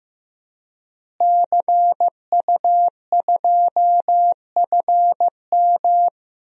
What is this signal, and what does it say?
Morse code, heard as a keyed tone, about 15 wpm, 700 Hz. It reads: CU2FM
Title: DXCC-100 Call Signs - 5 Characters - Repeated 3x - 15wpm